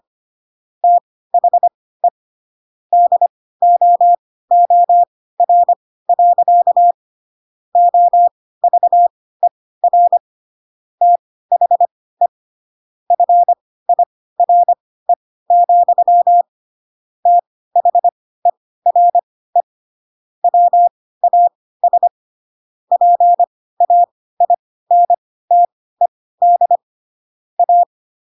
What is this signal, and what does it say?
Morse code, heard as a keyed tone, about 25 wpm, spaced as for 16 wpm, 700 Hz. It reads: THE DOOR. OVER THE FIRE, THERE WAS PAINTED A
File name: Pinocchio / Pinocchio ch3 10 25 ik1hge